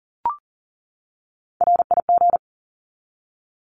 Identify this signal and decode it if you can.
Morse code, heard as a keyed tone, about 40 wpm, 700 Hz. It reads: RIG
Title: Abbreviations 40wpm